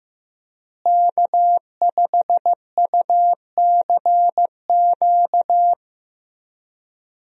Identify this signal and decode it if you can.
Morse code, heard as a keyed tone, about 15 wpm, 700 Hz. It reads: K5UCQ
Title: USA Call Signs - 5 Characters - Repeated 3x - 15wpm